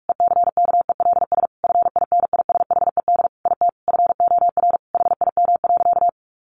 Morse code, heard as a keyed tone, about 44 wpm, 700 Hz. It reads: ECKELS FINISHED IT FOR HIM.
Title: A Sound Of Thunder 45wpm